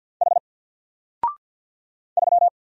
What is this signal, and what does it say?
Morse code, heard as a keyed tone, about 48 wpm, 700 Hz. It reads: H 3
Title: Single Character (Letter:Number:Punctuation:Prosign) 50wpm